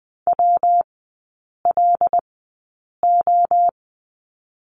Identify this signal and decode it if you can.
Morse code, heard as a keyed tone, about 20 wpm, 700 Hz. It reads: W L O
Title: 114 Lesson 19e - Letter w - Three Characters 20wpm